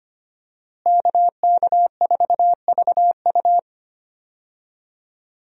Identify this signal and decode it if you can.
Morse code, heard as a keyed tone, about 25 wpm, 690 Hz. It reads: KK4VU
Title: USA Call Signs - 5 Characters - Repeated 2x - 25wpm